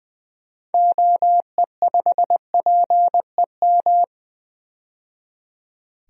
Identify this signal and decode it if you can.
Morse code, heard as a keyed tone, about 20 wpm, 700 Hz. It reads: OE5PEM
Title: DXCC-100 Call Signs - 6 Characters - 20wpm